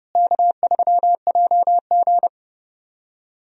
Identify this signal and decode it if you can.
Morse code, heard as a keyed tone, about 30 wpm, 700 Hz. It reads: K3JG
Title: USA Call Signs - 4 Characters - Repeated 3x - 30wpm